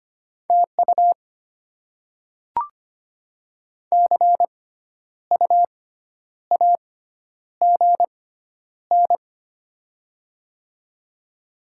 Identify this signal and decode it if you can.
Morse code, heard as a keyed tone, about 25 wpm, spaced as for 10 wpm, 700 Hz. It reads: TU CU AGN
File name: Top 20 QSO Elements - 6x Character Spacing - 25wpm